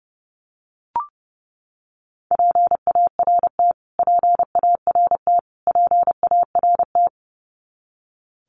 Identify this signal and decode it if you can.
Morse code, heard as a keyed tone, about 29 wpm, 700 Hz. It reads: PART PART PART
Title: Top 500 Words Repeated 30wpm